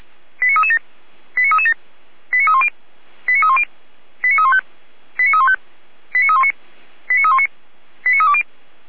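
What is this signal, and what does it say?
Five-tone selective call (ZVEI), heard as quick runs of five tones, tones of about 70 ms. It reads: 89337 89337 89310 89310 89315 89315 89319 89319 89320
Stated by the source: ZVEI five-tone selective call 89304 received on SDR.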